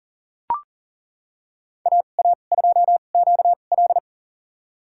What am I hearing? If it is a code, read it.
Morse code, heard as a keyed tone, about 37 wpm, spaced as for 28 wpm, 700 Hz. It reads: AA2QL
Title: USA Call Signs - 5 Characters - 2x Character Spacing - 40wpm